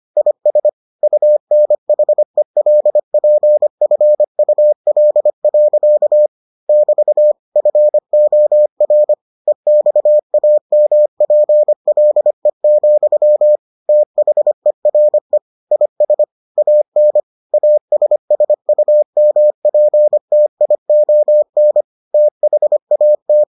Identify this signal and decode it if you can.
Morse code, heard as a keyed tone, about 25 wpm, 600 Hz. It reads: IS UNHELPFUL. <BT> FOR EXAMPLE, THERE IS AN ASSUMPTION THAT